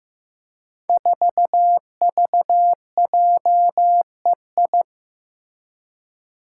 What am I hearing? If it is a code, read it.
Morse code, heard as a keyed tone, about 15 wpm, 700 Hz. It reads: 4VJEI
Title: Five Challenging Letter-Number Combinations 15wpm